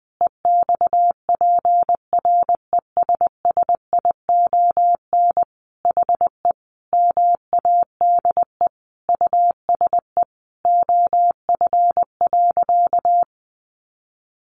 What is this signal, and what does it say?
Morse code, heard as a keyed tone, about 20 wpm, 700 Hz. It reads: EXPRESSION HE MADE USE OF.
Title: Sentences from Top 1200 Words 20wpm